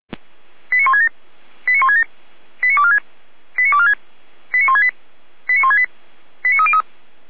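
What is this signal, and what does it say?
Five-tone selective call (ZVEI), heard as quick runs of five tones, tones of about 70 ms. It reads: 89167 89167 89246 89246 89168 89168 89303
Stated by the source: ZVEI five-tone call 89230 received by SDR, archived on the Feuerwehr Orlamünde fire brigade server.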